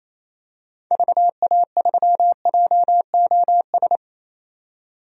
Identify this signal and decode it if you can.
Morse code, heard as a keyed tone, about 28 wpm, 700 Hz. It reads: VA3JOS
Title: DXCC-100 Call Signs - 6 Characters - 28wpm